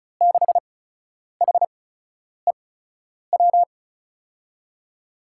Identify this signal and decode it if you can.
Morse code, heard as a keyed tone, about 36 wpm, spaced as for 10 wpm, 700 Hz. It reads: 6HEW
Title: USA Call Signs - 8x Character Spacing - 35wpm